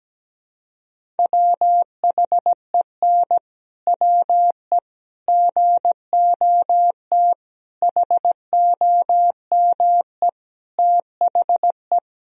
Morse code, heard as a keyed tone, about 17 wpm, 700 Hz. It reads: WHEN WE GOT HOME THE
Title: Sentences from Top 500 Words and Abbreviations 17wpm